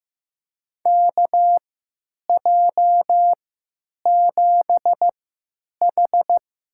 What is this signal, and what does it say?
Morse code, heard as a keyed tone, about 15 wpm, spaced as for 9 wpm, 700 Hz. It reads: KJ7H
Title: USA Call Signs - 6 Characters - 3x Character Spacing - 15wpm